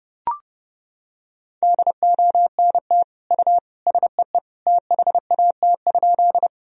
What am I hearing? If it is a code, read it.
Morse code, heard as a keyed tone, about 30 wpm, 700 Hz. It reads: DONT U SEE THAT?
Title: Sentences from Top 300 Words and Abbreviations 30wpm